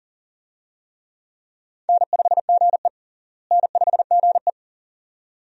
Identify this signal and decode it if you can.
Morse code, heard as a keyed tone, about 37 wpm, 700 Hz. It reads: N5GE N5GE
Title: USA Call Signs - 4 Characters - Repeated 2x - 40wpm